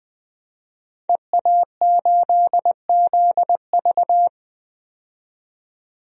Morse code, heard as a keyed tone, about 20 wpm, 700 Hz. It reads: EA8ZV
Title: DXCC-100 Call Signs - 5 Characters - 20wpm